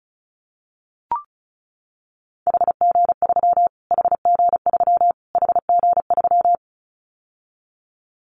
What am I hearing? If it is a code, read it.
Morse code, heard as a keyed tone, about 34 wpm, 700 Hz. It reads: HG3 HG3 HG3
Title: Callsign Prefixes for Top 25 DXCC - Repeated 3x - 35wpm